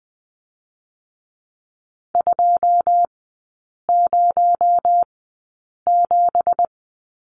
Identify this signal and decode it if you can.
Morse code, heard as a keyed tone, about 20 wpm, 700 Hz. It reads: 2 0 7
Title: 218 Lesson 35e - Number 0 - Three Characters 20wpm